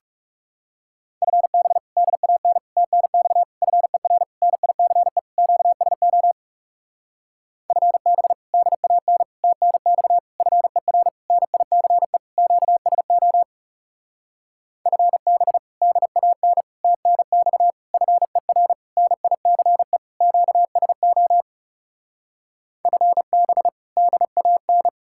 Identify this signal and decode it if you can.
Morse code, heard as a keyed tone, about 36 wpm, 700 Hz. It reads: FB DAN TNX FER NICE QSO FB DAN TNX FER NICE QSO FB DAN TNX FER NICE QSO FB DAN
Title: Speed-Racing Common QSO Phrases Encore - 30wpm